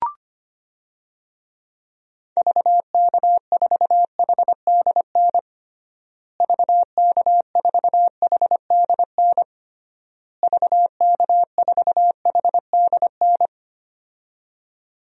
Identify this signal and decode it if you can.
Morse code, heard as a keyed tone, about 25 wpm, 700 Hz. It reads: VK4HDN VK4HDN VK4HDN
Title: DXCC-100 Call Signs - 6 Characters - Repeated 3x - 25wpm